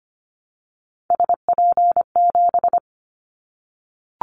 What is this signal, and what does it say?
Morse code, heard as a keyed tone, about 25 wpm, 700 Hz. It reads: SP7
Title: Callsign Prefixes for Top 25 DXCC - 25wpm